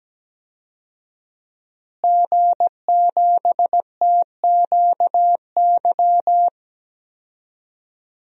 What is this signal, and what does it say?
Morse code, heard as a keyed tone, about 17 wpm, 700 Hz. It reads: G7TQY
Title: DXCC-100 Call Signs - 5 Characters - Repeated 3x - 17wpm